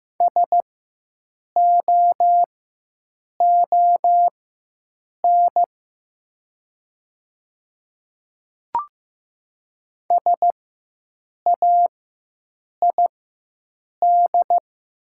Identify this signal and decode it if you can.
Morse code, heard as a keyed tone, about 15 wpm, spaced as for 7 wpm, 700 Hz. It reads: SOON SAID
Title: 4x Character Spacing - Top 500 Words - 15wpm